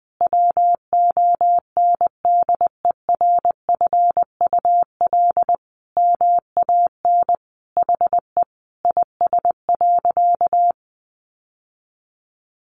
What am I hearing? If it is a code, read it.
Morse code, heard as a keyed tone, about 19 wpm, 700 Hz. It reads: WONDERFUL MAN HE IS.
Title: Sentences from Top 1100 Words 20wpm